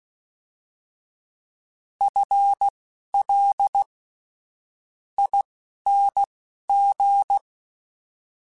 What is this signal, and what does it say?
Morse code, heard as a keyed tone, about 16 wpm, spaced as for 11 wpm, 800 Hz. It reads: FL ING
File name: FL / FL mp3s / FL Prefixes 3x2 sp @10